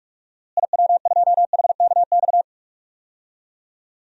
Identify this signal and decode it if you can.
Morse code, heard as a keyed tone, about 46 wpm, 700 Hz. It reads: IW2HKX
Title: DXCC-100 Call Signs - 6 Characters - Repeated 3x - 45wpm